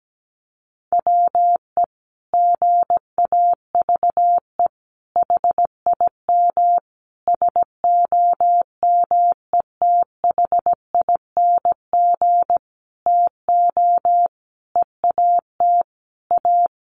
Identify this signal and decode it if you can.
Morse code, heard as a keyed tone, about 17 wpm, 700 Hz. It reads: WE GAVE HIM SOMETHING TO EAT A
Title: Sentences from Top 600 Words 17wpm